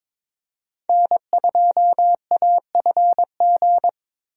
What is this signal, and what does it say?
Morse code, heard as a keyed tone, about 21 wpm, 700 Hz. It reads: N2AFG
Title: USA Call Signs - 5 Characters - Repeated 3x - 22wpm